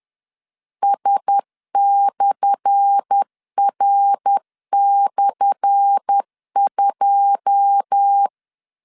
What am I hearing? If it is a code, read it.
Morse code, heard as a keyed tone, about 10 wpm, 800 Hz. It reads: S/R/2
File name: LEZIONI / 10 SIGNS / 10A SLASH 10wpm 2808